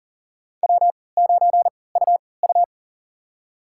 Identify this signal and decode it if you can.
Morse code, heard as a keyed tone, about 40 wpm, spaced as for 23 wpm, 700 Hz. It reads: W9UU
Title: USA Call Signs - 3x Character Spacing - 40wpm